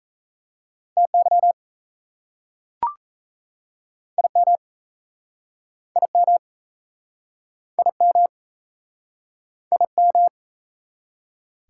Morse code, heard as a keyed tone, about 38 wpm, 700 Hz. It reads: TY IM IM IM IM